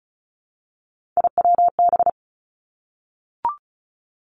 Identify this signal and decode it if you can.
Morse code, heard as a keyed tone, about 36 wpm, 700 Hz. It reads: IWB
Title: Three Challenging Letter-Number Combinations 35wpm